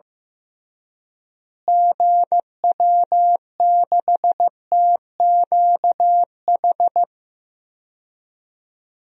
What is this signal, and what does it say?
Morse code, heard as a keyed tone, about 15 wpm, 700 Hz. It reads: GW6TQH
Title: DXCC-100 Call Signs - 6 Characters - Repeated 2x - 15wpm